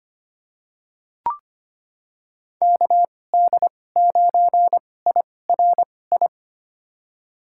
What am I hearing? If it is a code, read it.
Morse code, heard as a keyed tone, about 25 wpm, spaced as for 18 wpm, 700 Hz. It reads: KD9IRI
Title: USA Call Signs - 6 Characters - 2x Character Spacing - 25wpm